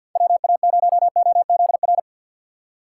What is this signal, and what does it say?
Morse code, heard as a keyed tone, about 53 wpm, 700 Hz. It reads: WA0OZR
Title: USA Call Signs - 6 Characters - 8x Character Spacing - 50wpm